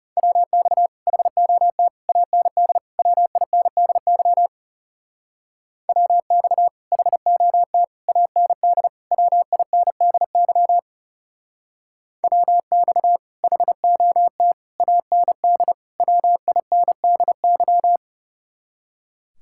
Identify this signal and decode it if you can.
Morse code, heard as a keyed tone, about 35 wpm, 700 Hz. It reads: WX HOT AND WINDY WX HOT AND WINDY WX HOT AND WINDY
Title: Speed-Racing Common QSO Phrases - 30wpm